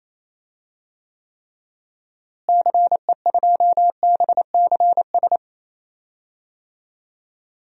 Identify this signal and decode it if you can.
Morse code, heard as a keyed tone, about 28 wpm, 700 Hz. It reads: CE2BCS
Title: DXCC-100 Call Signs - 6 Characters - Repeated 2x - 28wpm